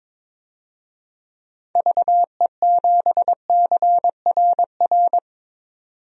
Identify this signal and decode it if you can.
Morse code, heard as a keyed tone, about 22 wpm, 700 Hz. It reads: VE7CRR
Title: DXCC-100 Call Signs - 6 Characters - 22wpm